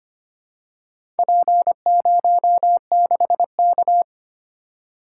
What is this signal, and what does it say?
Morse code, heard as a keyed tone, about 25 wpm, 700 Hz. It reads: P06K
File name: practice / 25wpm / Four Letter-Number Combinations - 25wpm